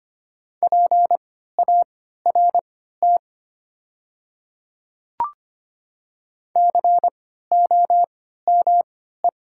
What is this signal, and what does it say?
Morse code, heard as a keyed tone, about 25 wpm, spaced as for 14 wpm, 700 Hz. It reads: PART COME
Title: 3x Character Spacing - Top 200 Words - 25wpm